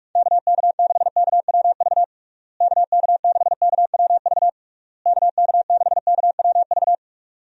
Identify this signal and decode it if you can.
Morse code, heard as a keyed tone, about 46 wpm, 700 Hz. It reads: KK6KWV KK6KWV KK6KWV
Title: USA Call Signs - 6 Characters - Repeated 3x - 45wpm